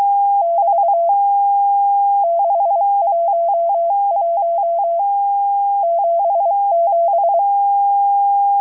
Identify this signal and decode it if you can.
Morse code, heard as a keyed tone, about 23 wpm, 680 Hz. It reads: <BT> 611 77
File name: HM02- -0437utc-7351khz